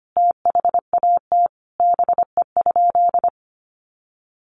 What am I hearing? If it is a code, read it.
Morse code, heard as a keyed tone, about 25 wpm, 700 Hz. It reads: THAT BE?
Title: Sentences from Top 100 Words 25wpm